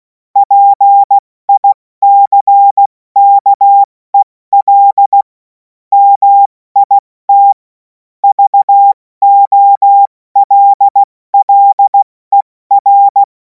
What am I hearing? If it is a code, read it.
Morse code, heard as a keyed tone, about 16 wpm, 800 Hz. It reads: PICKEL MIT VOLLER
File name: Morsekurs / Witze / Witz 17. Februar 2026 70 F80